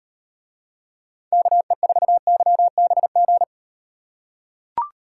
Morse code, heard as a keyed tone, about 38 wpm, 700 Hz. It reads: KE4YBG